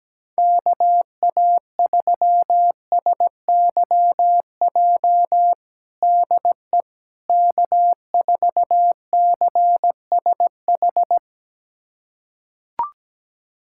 Morse code, heard as a keyed tone, about 17 wpm, 700 Hz. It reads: KA3SYJ DE K4CSH
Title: USA Callsign DE Callsign 17wpm